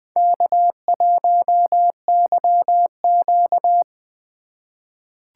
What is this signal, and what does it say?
Morse code, heard as a keyed tone, about 20 wpm, 700 Hz. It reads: K1YQ
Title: USA Call Signs - 4 Characters - 2x Character Spacing - 20wpm